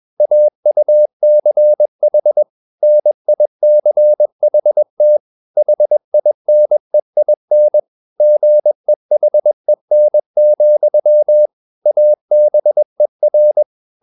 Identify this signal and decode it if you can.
Morse code, heard as a keyed tone, about 21 wpm, 600 Hz. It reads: AUCH NICHT HINEIN GEHEN, ABER